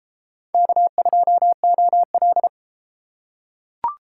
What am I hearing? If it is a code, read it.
Morse code, heard as a keyed tone, about 32 wpm, 700 Hz. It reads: K2OL